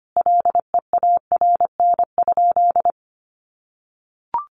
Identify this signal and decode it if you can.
Morse code, heard as a keyed tone, about 25 wpm, 700 Hz. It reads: LEARN?
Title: Sentences from Top 1200 Words 25wpm